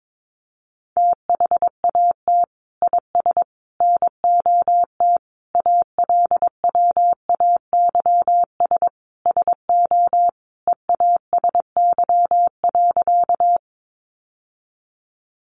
Morse code, heard as a keyed tone, about 22 wpm, 700 Hz. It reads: THAT IS NOT ALWAYS SO EASY.
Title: Sentences from Top 700 Words 22wpm